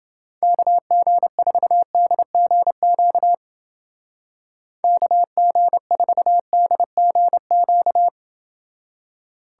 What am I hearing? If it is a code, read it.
Morse code, heard as a keyed tone, about 29 wpm, 700 Hz. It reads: KG4DGQ KG4DGQ